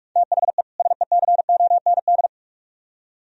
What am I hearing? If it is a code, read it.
Morse code, heard as a keyed tone, about 56 wpm, 700 Hz. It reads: THE SECOND
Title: Speed-Racing - Sets of 2 Words 45wpm